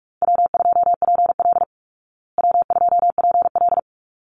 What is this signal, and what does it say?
Morse code, heard as a keyed tone, about 44 wpm, 700 Hz. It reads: W2PL W2PL
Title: Easy Call Signs Repeated 2x - 45wpm